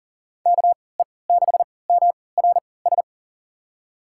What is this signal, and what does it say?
Morse code, heard as a keyed tone, about 40 wpm, spaced as for 23 wpm, 700 Hz. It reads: KE6MRS
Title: USA Call Signs - 6 Characters - 3x Character Spacing - 40wpm